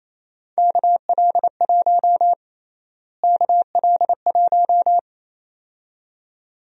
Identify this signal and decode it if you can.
Morse code, heard as a keyed tone, about 28 wpm, 700 Hz. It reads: KL1 KL1
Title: USA Call Sign Prefixes - Repeated 2x - 28wpm